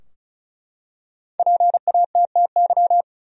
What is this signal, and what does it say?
Morse code, heard as a keyed tone, about 36 wpm, 700 Hz. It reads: PATTY
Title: USA Common Names - Max 5 Letters - 35wpm